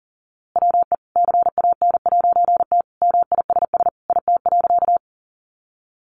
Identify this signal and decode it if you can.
Morse code, heard as a keyed tone, about 37 wpm, 700 Hz. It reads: WE CAN'T MISS IT.
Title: A Sound Of Thunder 40wpm